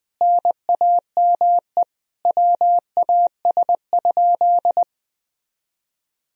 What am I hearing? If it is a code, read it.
Morse code, heard as a keyed tone, about 20 wpm, 700 Hz. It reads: NAME WAS?
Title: Sentences from Top 200 Words and Abbreviations 20wpm